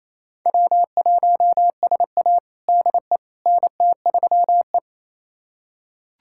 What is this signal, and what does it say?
Morse code, heard as a keyed tone, about 28 wpm, 700 Hz. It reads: W1SA DE NT3E
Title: USA Callsign DE Callsign - 4 Characters - 28wpm